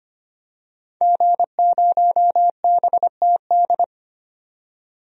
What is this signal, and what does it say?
Morse code, heard as a keyed tone, about 25 wpm, 700 Hz. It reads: G0BTD
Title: DXCC-100 Call Signs - 5 Characters - 25wpm